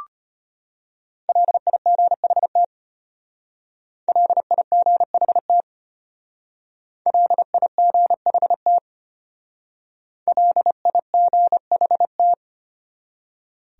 Morse code, heard as a keyed tone, about 32 wpm, 700 Hz. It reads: LIGHT LIGHT LIGHT LIGHT